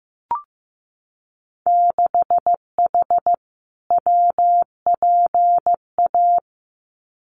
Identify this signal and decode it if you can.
Morse code, heard as a keyed tone, about 15 wpm, 700 Hz. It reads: 6H WPA
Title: Winter Field Day Exchanges Encore 15wpm